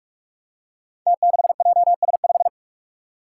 Morse code, heard as a keyed tone, about 46 wpm, 700 Hz. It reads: T6JS5
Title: Five Challenging Letter-Number Combinations 45wpm